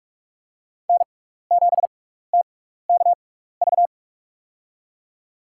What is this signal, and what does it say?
Morse code, heard as a keyed tone, about 47 wpm, spaced as for 16 wpm, 700 Hz. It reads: N7TKV
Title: USA Call Signs - 5 Characters - 6x Character Spacing - 45wpm